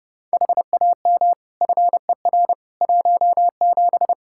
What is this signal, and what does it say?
Morse code, heard as a keyed tone, about 31 wpm, 700 Hz. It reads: HAM FER 17
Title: Practice QSO - Exchange 3 - More Info - 30wpm